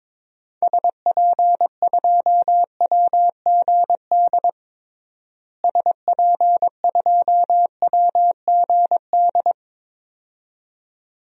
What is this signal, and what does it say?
Morse code, heard as a keyed tone, about 22 wpm, 700 Hz. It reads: SP2WGD SP2WGD
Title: DXCC-100 Call Signs - 6 Characters - Repeated 2x - 22wpm